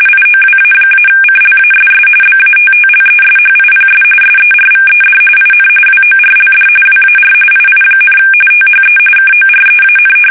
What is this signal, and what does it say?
RTTY, about 45.45 baud. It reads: IOWA-CLASS BATTLESHIPS SERVED IN WORLD WAR II, THE KOREAN
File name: fsk850 / battleshipnj